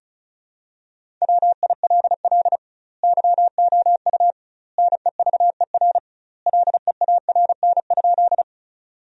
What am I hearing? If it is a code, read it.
Morse code, heard as a keyed tone, about 36 wpm, 700 Hz. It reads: WILL YOU NEVER LEARN?
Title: 2x Word-Spacing - Sentences from Top 300 Words - 35wpm